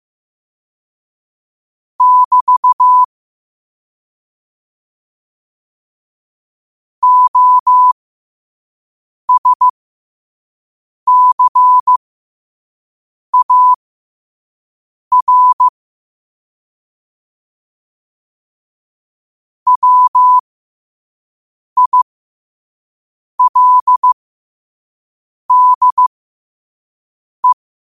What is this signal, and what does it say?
Morse code, heard as a keyed tone, about 15 wpm, spaced as for 5 wpm, 1000 Hz. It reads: <BT> OSCAR WILDE
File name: qotdfiles / 5wpm